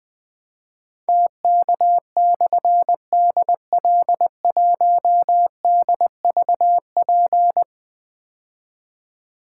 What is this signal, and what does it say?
Morse code, heard as a keyed tone, about 20 wpm, 700 Hz. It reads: TK/DL1DVP
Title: Difficult Call Signs 20wpm